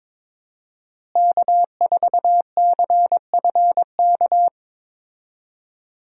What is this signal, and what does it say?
Morse code, heard as a keyed tone, about 22 wpm, 700 Hz. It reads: K4CFK
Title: USA Call Signs - 5 Characters - Repeated 3x - 22wpm